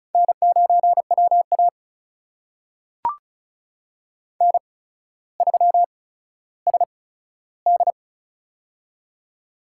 Morse code, heard as a keyed tone, about 36 wpm, spaced as for 15 wpm, 700 Hz. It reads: N9WA N3SD
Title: USA Call Signs - 4 Characters - 8x Character Spacing - 35wpm